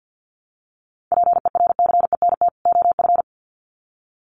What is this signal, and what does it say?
Morse code, heard as a keyed tone, about 50 wpm, 700 Hz. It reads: PERCENT OF
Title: Sets of 2 Words 50wpm